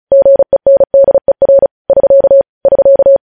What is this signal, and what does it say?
Morse code, heard as a keyed tone, about 36 wpm, 550 Hz. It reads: GENDER <SK> <SK>
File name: podcast / News headlines-1x-35x35